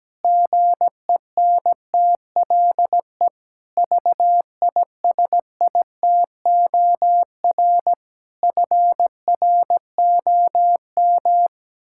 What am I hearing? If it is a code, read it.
Morse code, heard as a keyed tone, about 17 wpm, 700 Hz. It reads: GENTLE VISITOR FROM
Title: Pangram Challenge v2 - 17wpm